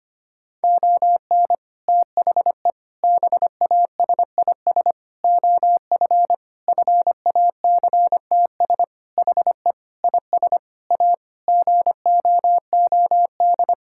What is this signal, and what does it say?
Morse code, heard as a keyed tone, about 25 wpm, 700 Hz. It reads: ON THE BASIS OF FACTS HE IS A GOOD
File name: practice / 25wpm / Sentences with 3-Word Phrases - 25wpm